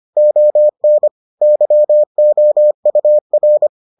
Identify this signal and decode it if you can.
Morse code, heard as a keyed tone, about 25 wpm, 600 Hz. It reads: ON YOUR